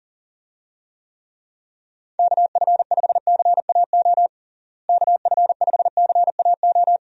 Morse code, heard as a keyed tone, about 40 wpm, 700 Hz. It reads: KF5CAO KF5CAO
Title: USA Call Signs - 6 Characters - Repeated 3x - 40wpm